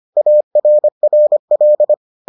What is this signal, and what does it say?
Morse code, heard as a keyed tone, about 25 wpm, 600 Hz. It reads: ARRL